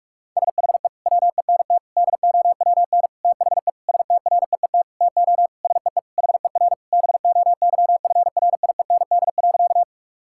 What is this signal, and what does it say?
Morse code, heard as a keyed tone, about 46 wpm, 690 Hz. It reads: SHE WENT DOWN THE STREET TO SEE HER BOYFRIEND.
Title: Sentences from Top 1000 Words - Review Part 5 - 45wpm